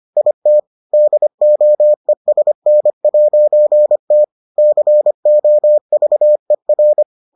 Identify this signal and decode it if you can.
Morse code, heard as a keyed tone, about 25 wpm, 600 Hz. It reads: IT DOESN'T COVER